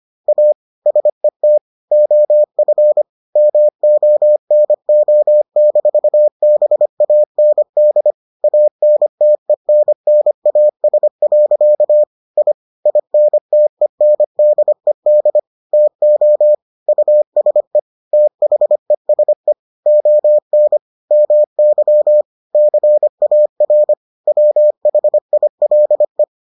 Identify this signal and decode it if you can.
Morse code, heard as a keyed tone, about 25 wpm, 600 Hz. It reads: A SET OF MONO-BAND ANTENNAS. I INTENDED TO USE THESE ON MY CAR WHILE